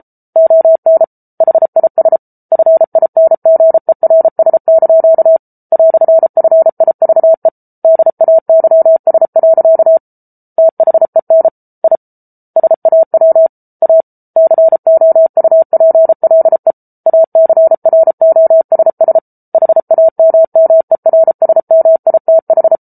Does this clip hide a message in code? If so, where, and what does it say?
Yes: Morse code from 0.4 to 22.8 s, heard as a keyed tone, about 34 wpm, 650 Hz. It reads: ON HIS FINGERS) "FIVE DAYS. THEN I SAW A COUPLE ACROSS HAMMERSMITH